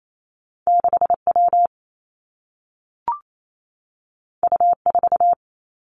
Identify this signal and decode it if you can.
Morse code, heard as a keyed tone, about 28 wpm, 700 Hz. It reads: 6 W U 4
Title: Two Challenging Letter-Number Combinations 28wpm